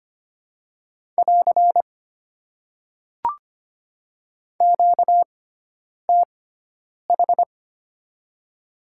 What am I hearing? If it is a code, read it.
Morse code, heard as a keyed tone, about 25 wpm, spaced as for 9 wpm, 700 Hz. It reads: <AR> QTH
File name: Top 90 QSO Elements - 6x Character Spacing - 25wpm